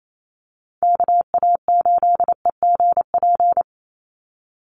Morse code, heard as a keyed tone, about 28 wpm, 700 Hz. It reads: KA8EGP
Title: USA Call Signs 28wpm